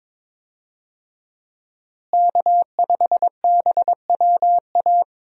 Morse code, heard as a keyed tone, about 22 wpm, 700 Hz. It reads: K5BWA
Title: USA Call Signs - 5 Characters - Repeated 3x - 22wpm